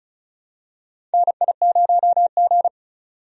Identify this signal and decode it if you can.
Morse code, heard as a keyed tone, about 36 wpm, 700 Hz. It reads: NI0G
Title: USA Call Signs - 4 Characters - 35wpm